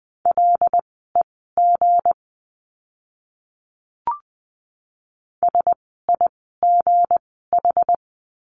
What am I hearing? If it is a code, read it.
Morse code, heard as a keyed tone, about 20 wpm, spaced as for 14 wpm, 700 Hz. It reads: LEG SIGH
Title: 160 Lesson 26f - Letter g - Words 20wpm